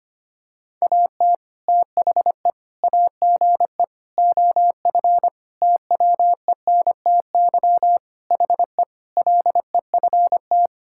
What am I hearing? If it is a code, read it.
Morse code, heard as a keyed tone, about 25 wpm, 700 Hz. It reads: AT THE AGE OF TWENTY HE LEFT
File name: practice / 25wpm / Sentences with 4-Word Phrases - 25wpm